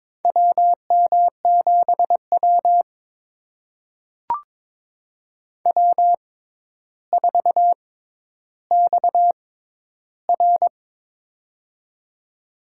Morse code, heard as a keyed tone, about 22 wpm, spaced as for 11 wpm, 700 Hz. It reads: WM7W W4XR
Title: USA Call Signs - 4 Characters - 6x Character Spacing - 22wpm